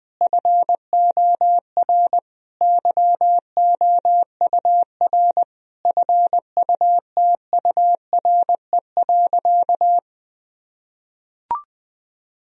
Morse code, heard as a keyed tone, about 20 wpm, 700 Hz. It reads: FOR YOUR FUTURE.
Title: Sentences from Top 1000 Words - Review Part 4 - 20wpm